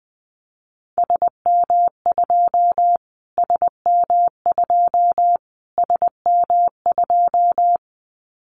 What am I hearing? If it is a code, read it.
Morse code, heard as a keyed tone, about 20 wpm, 700 Hz. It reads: SM2 SM2 SM2
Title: Callsign Prefixes for Top 25 DXCC - Repeated 3x - 20wpm